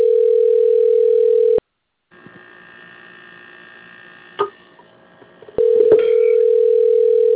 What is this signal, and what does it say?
Ringback tone (North America), 440 + 480 Hz.
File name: out- - - - - .4915